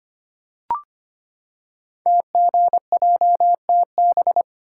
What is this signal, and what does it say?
Morse code, heard as a keyed tone, about 25 wpm, 700 Hz. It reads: TGJTB
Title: Code Groups - 5 Letters - 25wpm